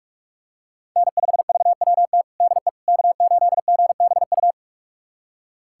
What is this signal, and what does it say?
Morse code, heard as a keyed tone, about 46 wpm, 700 Hz. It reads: N5VWT DE K8GBU
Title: USA Callsign DE Callsign - 5 Characters - 45wpm